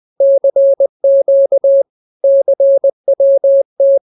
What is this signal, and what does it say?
Morse code, heard as a keyed tone, about 20 wpm, 550 Hz. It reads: CQ CWT